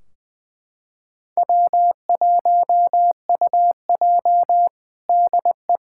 Morse code, heard as a keyed tone, about 20 wpm, 700 Hz. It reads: W1UJ DE
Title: Practice QSO - Exchange 4 - Ending - 20wpm